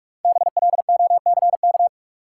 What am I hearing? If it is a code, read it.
Morse code, heard as a keyed tone, about 47 wpm, 700 Hz. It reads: BLOCK
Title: Sentences from Top 1600 Words - New Word Drill - 45wpm